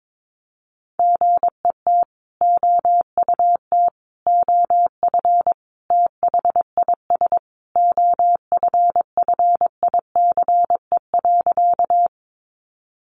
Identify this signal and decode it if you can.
Morse code, heard as a keyed tone, about 22 wpm, 700 Hz. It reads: GET OUT OF THIS OFFICE.
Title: Sentences from Top 1000 Words 22wpm